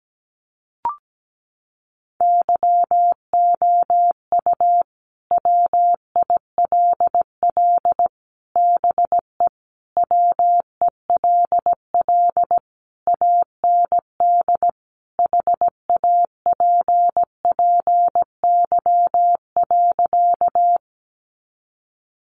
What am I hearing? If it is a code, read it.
Morse code, heard as a keyed tone, about 17 wpm, 700 Hz. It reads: YOU WILL BE WELL AND HAPPY.
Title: Sentences from Top 1000 Words 17wpm